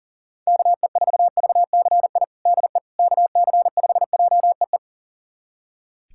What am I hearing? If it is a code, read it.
Morse code, heard as a keyed tone, about 40 wpm, 700 Hz. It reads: KE4VCI DE KC5JEE
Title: USA Callsign DE Callsign - 6 Characters - 40wpm